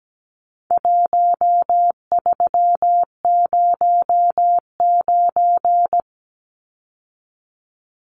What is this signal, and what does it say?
Morse code, heard as a keyed tone, about 17 wpm, 700 Hz. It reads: 1309
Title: Numbers - 4-Digit 17wpm